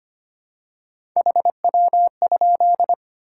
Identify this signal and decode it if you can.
Morse code, heard as a keyed tone, about 25 wpm, 700 Hz. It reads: HW?
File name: practice / 25wpm / Top 20 QSO Elements - 25wpm